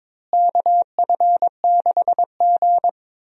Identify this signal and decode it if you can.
Morse code, heard as a keyed tone, about 22 wpm, 700 Hz. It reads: KF6G
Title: USA Call Signs - 4 Characters - Repeated 3x - 22wpm